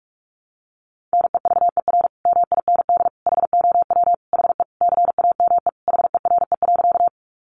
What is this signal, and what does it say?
Morse code, heard as a keyed tone, about 44 wpm, 700 Hz. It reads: NEVER MIND HOW HE CAME HERE.
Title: Sentences from Top 500 Words 45wpm